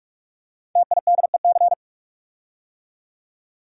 Morse code, heard as a keyed tone, about 44 wpm, 700 Hz. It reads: TIDEC
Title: Code Groups - 5 Letters - 45wpm